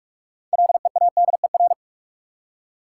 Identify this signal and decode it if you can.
Morse code, heard as a keyed tone, about 46 wpm, 700 Hz. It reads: LEADER
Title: Six Letter Words - 45wpm